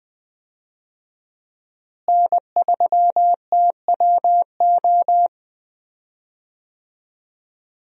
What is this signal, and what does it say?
Morse code, heard as a keyed tone, about 20 wpm, 700 Hz. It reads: N3TWO
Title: USA Call Signs - 5 Characters - Repeated 2x - 20wpm